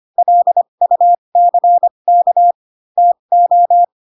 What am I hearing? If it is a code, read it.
Morse code, heard as a keyed tone, about 25 wpm, spaced as for 22 wpm, 700 Hz. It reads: LUCK TO
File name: Pinocchio / Pinocchio ch2 18 25 ik1hge